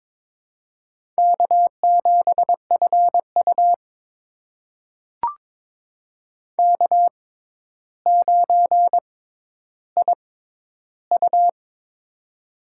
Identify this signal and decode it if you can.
Morse code, heard as a keyed tone, about 22 wpm, spaced as for 11 wpm, 700 Hz. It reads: K7FU K9IU
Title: USA Call Signs - 4 Characters - 6x Character Spacing - 22wpm